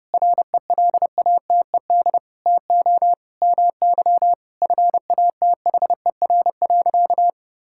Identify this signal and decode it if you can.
Morse code, heard as a keyed tone, about 30 wpm, 700 Hz. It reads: RELATED TO MY FATHER.
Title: Sentences from Top 1000 Words - Review Part 5 - 30wpm